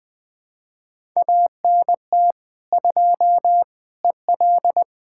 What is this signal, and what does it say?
Morse code, heard as a keyed tone, about 20 wpm, 700 Hz. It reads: ANT 2 EL
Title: Speed-Racing Common QSO Phrases - 20wpm